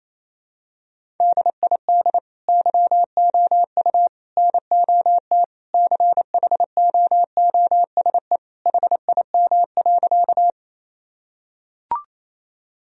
Morse code, heard as a keyed tone, about 28 wpm, 700 Hz. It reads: DID YOU NOT CHOOSE HIM.
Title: Sentences from Top 1000 Words - Review Part 4 - 28wpm